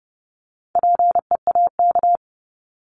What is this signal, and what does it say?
Morse code, heard as a keyed tone, about 30 wpm, 700 Hz. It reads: PEAK
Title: Top 1751-2000 Words 30wpm